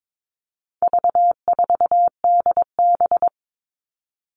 Morse code, heard as a keyed tone, about 22 wpm, 700 Hz. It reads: V4DB
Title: All Cut Numbers - 4-Digit 22wpm